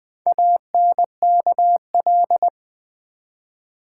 Morse code, heard as a keyed tone, about 20 wpm, 700 Hz. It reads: ANKL
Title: Build A Word - 1 Letter At A Time - 3 to 5-Letter Words - 20wpm